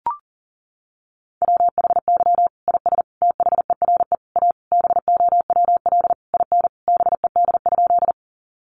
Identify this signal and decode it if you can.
Morse code, heard as a keyed tone, about 40 wpm, 700 Hz. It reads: WHY IS THERE A BOWL IN BED?
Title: Sentences from Top 1200 Words 40wpm